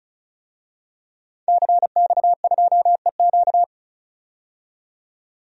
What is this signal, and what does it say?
Morse code, heard as a keyed tone, about 36 wpm, 700 Hz. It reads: CX2EQ
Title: DXCC-100 Call Signs - 5 Characters - 35wpm